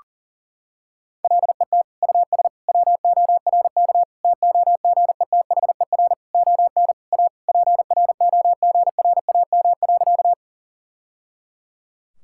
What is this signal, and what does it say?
Morse code, heard as a keyed tone, about 40 wpm, 700 Hz. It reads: LET US WORK TOGETHER ON A PROGRAM.
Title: Sentences from Top 1000 Words - Review Part 5 - 40wpm